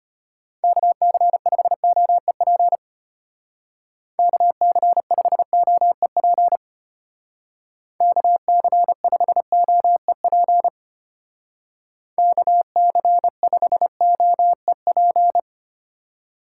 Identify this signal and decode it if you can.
Morse code, heard as a keyed tone, about 30 wpm, 700 Hz. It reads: KC5OEP KC5OEP KC5OEP KC5OEP
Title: Speed Racing - USA 2x3 Call Signs 25wpm